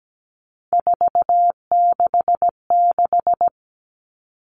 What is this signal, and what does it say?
Morse code, heard as a keyed tone, about 17 wpm, 700 Hz. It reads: 466
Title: All Cut Numbers - 3-Digit 17wpm